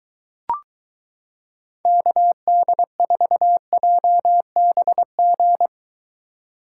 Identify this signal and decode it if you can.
Morse code, heard as a keyed tone, about 22 wpm, 700 Hz. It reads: KD4JBG